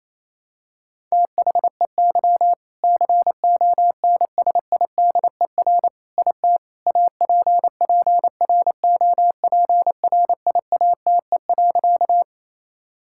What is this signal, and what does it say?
Morse code, heard as a keyed tone, about 28 wpm, 700 Hz. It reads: THEY CONSIDER IT APPROPRIATE.
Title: Sentences from Top 1000 Words - Review Part 5 - 28wpm